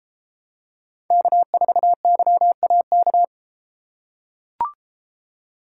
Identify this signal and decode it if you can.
Morse code, heard as a keyed tone, about 33 wpm, 700 Hz. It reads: K4YAK